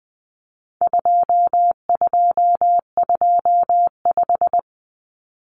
Morse code, heard as a keyed tone, about 20 wpm, 700 Hz. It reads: 2225
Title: Common Cut Numbers - 4-Digit 20wpm